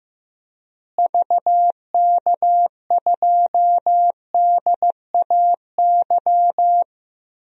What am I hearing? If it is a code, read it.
Morse code, heard as a keyed tone, about 15 wpm, 700 Hz. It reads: VK2DAY
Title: DXCC-100 Call Signs - 6 Characters - Repeated 3x - 15wpm